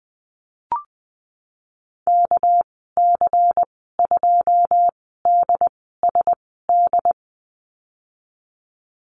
Morse code, heard as a keyed tone, about 19 wpm, spaced as for 14 wpm, 700 Hz. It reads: KC2DSD
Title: 200 Lesson 32g - Letter k - Call Signs 20wpm